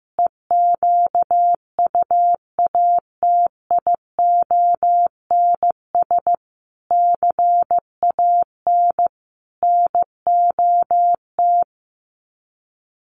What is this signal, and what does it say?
Morse code, heard as a keyed tone, about 15 wpm, 700 Hz. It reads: EQUATIONS CAN NOT
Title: Sets of 5 Words 15wpm